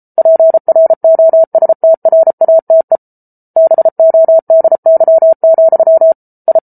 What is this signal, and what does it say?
Morse code, heard as a keyed tone, about 34 wpm, 650 Hz. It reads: PROSTRATE BODY, I